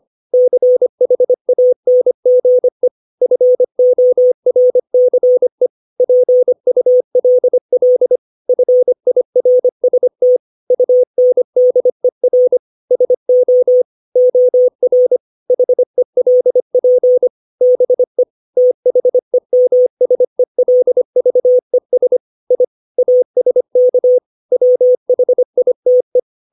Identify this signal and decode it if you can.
Morse code, heard as a keyed tone, about 25 wpm, 500 Hz. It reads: CHANGE FORCE PULL FIRST UNDER SO OR HELP BE THEMSELVES I ASK WHITE
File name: 632 words / 632 Words 25WPM